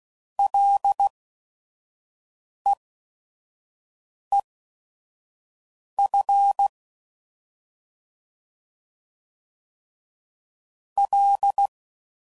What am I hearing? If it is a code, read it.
Morse code, heard as a keyed tone, about 16 wpm, spaced as for 5 wpm, 800 Hz. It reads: LEEF L
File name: Prefixes/FLs & THs/Single L's